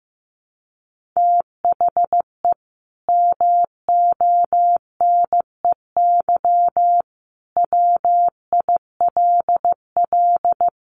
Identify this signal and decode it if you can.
Morse code, heard as a keyed tone, about 15 wpm, 700 Hz. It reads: THE MONEY WILL
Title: Sentences from Top 1500 Words 15wpm